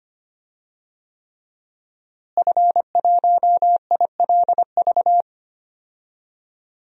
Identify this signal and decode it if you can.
Morse code, heard as a keyed tone, about 25 wpm, 700 Hz. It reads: F1ILV
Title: DXCC-100 Call Signs - 5 Characters - Repeated 3x - 25wpm